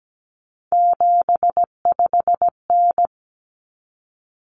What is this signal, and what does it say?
Morse code, heard as a keyed tone, about 17 wpm, 700 Hz. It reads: 75N
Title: Common Cut Numbers - 3-Digit 17wpm